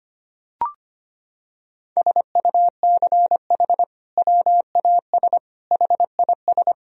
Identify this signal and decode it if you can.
Morse code, heard as a keyed tone, about 25 wpm, 700 Hz. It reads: SUCH WAS HIS
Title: Sentences from Top 1000 Words - Review Part 1 - 25wpm